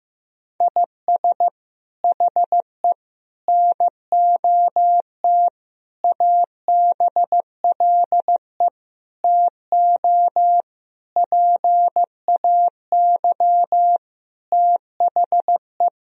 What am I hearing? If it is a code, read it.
Morse code, heard as a keyed tone, about 15 wpm, 700 Hz. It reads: IS HE NOT ABLE TO PAY THE
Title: Sentences from Top 1000 Words - Review Part 2 - 15wpm